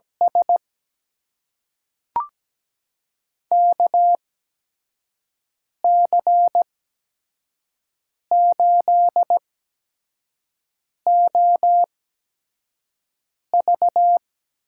Morse code, heard as a keyed tone, about 17 wpm, spaced as for 5 wpm, 700 Hz. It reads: S KC8OV
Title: USA Call Signs - 6 Characters - 8x Character Spacing - 17wpm